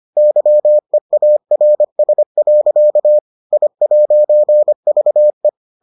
Morse code, heard as a keyed tone, about 25 wpm, 600 Hz. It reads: YEARS. I'VE